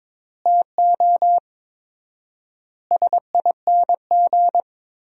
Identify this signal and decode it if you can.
Morse code, heard as a keyed tone, about 22 wpm, 700 Hz. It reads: TO SING
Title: 4x Word-Spacing - Sentences from Top 400 Words - 22wpm